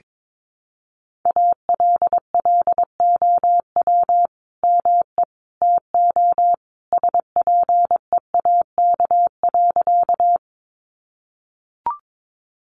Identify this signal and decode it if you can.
Morse code, heard as a keyed tone, about 22 wpm, 700 Hz. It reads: ALLOW ME TO SPEAK.
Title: Sentences from Top 1100 Words 22wpm